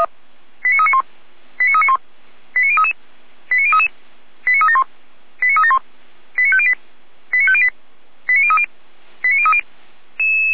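Five-tone selective call (ZVEI), heard as quick runs of five tones, tones of about 80 ms. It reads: 89391 89391 89033 89033 89371 89371 89508 89508 89030 89030
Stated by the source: ZVEI five-tone call received on an SDR.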